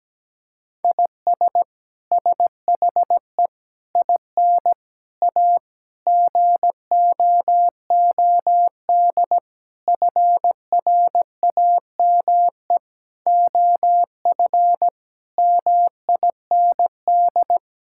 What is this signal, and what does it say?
Morse code, heard as a keyed tone, about 17 wpm, 700 Hz. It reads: IS SHE IN A GOOD FRAME OF MIND
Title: Sentences from Top 1000 Words - Review Part 6 - 17wpm